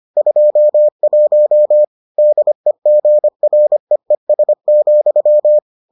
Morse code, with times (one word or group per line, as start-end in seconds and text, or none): 0.2-1.8 21
2.2-5.6 DEGREES,